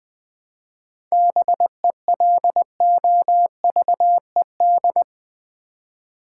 Morse code, heard as a keyed tone, about 20 wpm, 700 Hz. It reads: BELOVED
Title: 7 Letter Words - First 3 Letters Given - 20wpm